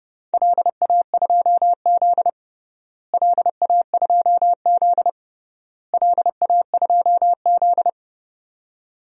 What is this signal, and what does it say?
Morse code, heard as a keyed tone, about 31 wpm, 700 Hz. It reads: LA2Z LA2Z LA2Z
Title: DXCC-100 Call Signs - 4 Characters - Repeated 3x - 30wpm